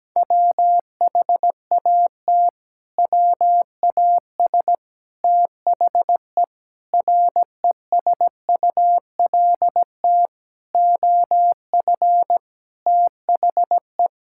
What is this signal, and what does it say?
Morse code, heard as a keyed tone, about 17 wpm, 700 Hz. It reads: WHAT WAS THE RESULT OF THE
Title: Sentences from Top 1000 Words - Review Part 5 - 17wpm